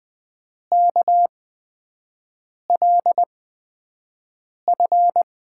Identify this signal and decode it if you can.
Morse code, heard as a keyed tone, about 20 wpm, spaced as for 5 wpm, 700 Hz. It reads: KLF
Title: USA Call Signs - 8x Character Spacing - 20wpm